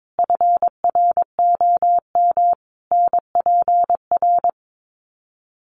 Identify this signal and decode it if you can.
Morse code, heard as a keyed tone, about 22 wpm, 700 Hz. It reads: FROM NPR
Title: Sets of 5 Words Encore 22wpm